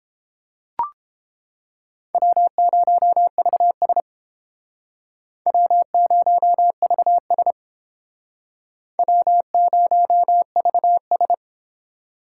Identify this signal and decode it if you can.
Morse code, heard as a keyed tone, about 30 wpm, 700 Hz. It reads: W0VS W0VS W0VS